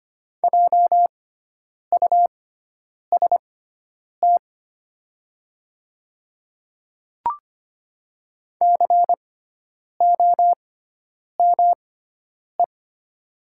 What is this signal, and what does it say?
Morse code, heard as a keyed tone, about 25 wpm, spaced as for 9 wpm, 700 Hz. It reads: JUST COME
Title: 6x Character Spacing - Top 300 Words - 25wpm